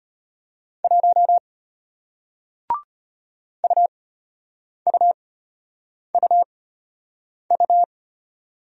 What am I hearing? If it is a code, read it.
Morse code, heard as a keyed tone, about 34 wpm, spaced as for 8 wpm, 700 Hz. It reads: 1 UUUU